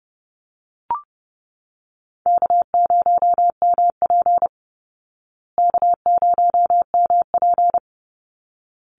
Morse code, heard as a keyed tone, about 30 wpm, 700 Hz. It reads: K0MP K0MP
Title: Call Signs Repeated 3x - 30wpm